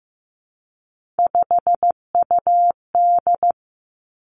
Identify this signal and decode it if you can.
Morse code, heard as a keyed tone, about 15 wpm, 700 Hz. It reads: 5UD
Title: Three Challenging Letter-Number Combinations 15wpm